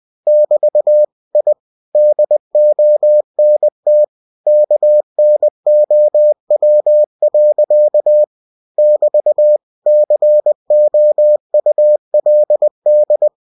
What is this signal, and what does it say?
Morse code, heard as a keyed tone, about 20 wpm, 600 Hz. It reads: <BT> I DONT KNOW. <BT> COULD
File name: cw-books / winnie-the-pooh / winnie-20wpm-page-0110